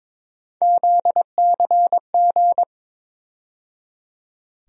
Morse code, heard as a keyed tone, about 22 wpm, 700 Hz. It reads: ZCG
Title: USA Call Sign Postfixes - 3 Letters 22wpm